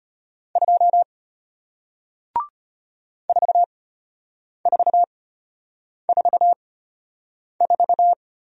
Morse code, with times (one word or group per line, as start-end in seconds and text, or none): 0.6-1.0 2
3.3-8.1 4444